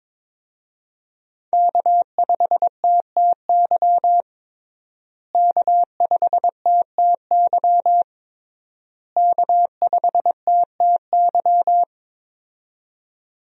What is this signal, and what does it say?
Morse code, heard as a keyed tone, about 22 wpm, 700 Hz. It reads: K5TTY K5TTY K5TTY
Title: USA Call Signs - 5 Characters - Repeated 3x - 22wpm